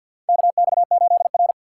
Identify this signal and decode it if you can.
Morse code, heard as a keyed tone, about 50 wpm, 700 Hz. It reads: KX8R
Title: USA Call Signs - 4 Characters - 3x Character Spacing - 50wpm